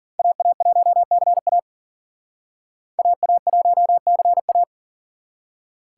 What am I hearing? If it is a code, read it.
Morse code, heard as a keyed tone, about 53 wpm, 700 Hz. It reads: AA1CA AA1CA